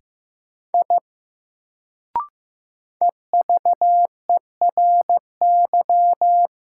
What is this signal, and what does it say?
Morse code, heard as a keyed tone, about 15 wpm, 700 Hz. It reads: I EVERY
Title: Sentences from Top 200 Words - New Word Drill - 15wpm